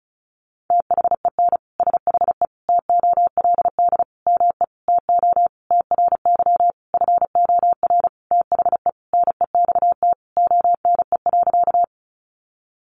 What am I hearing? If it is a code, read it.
Morse code, heard as a keyed tone, about 34 wpm, 700 Hz. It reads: THEN SHE TOLD ME TO TRY FOR THE NEXT ONE.
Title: Sentences from Top 400 Words 35wpm